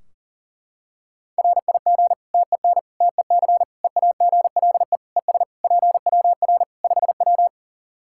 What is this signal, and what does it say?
Morse code, heard as a keyed tone, about 40 wpm, 700 Hz. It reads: RIG TEN TEC EAGLE ES PWR 5W
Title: Practice QSO - Exchange 2 - Info - 40wpm